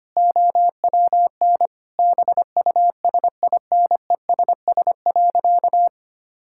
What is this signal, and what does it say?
Morse code, heard as a keyed tone, about 25 wpm, 700 Hz. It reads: OWN BUSINESS.
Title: Sentences from Top 1000 Words - Review Part 6 - 25wpm